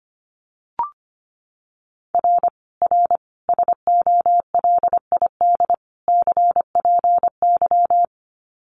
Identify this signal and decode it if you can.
Morse code, heard as a keyed tone, about 25 wpm, 700 Hz. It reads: R R SOLID CPY
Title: Common QSO Phrases 25wpm